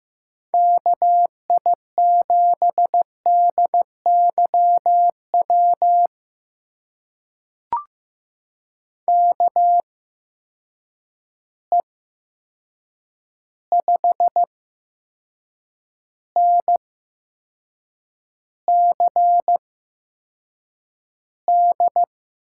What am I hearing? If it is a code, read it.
Morse code, heard as a keyed tone, about 15 wpm, spaced as for 6 wpm, 700 Hz. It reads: KI7DYW KE5NCD
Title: USA Call Signs - 6 Characters - 8x Character Spacing - 15wpm